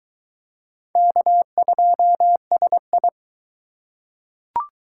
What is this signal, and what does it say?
Morse code, heard as a keyed tone, about 20 wpm, 700 Hz. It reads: K2SI